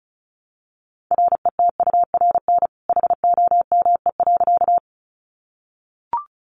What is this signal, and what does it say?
Morse code, heard as a keyed tone, about 34 wpm, 700 Hz. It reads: RETURN HOME.
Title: Sentences from Top 800 Words 35wpm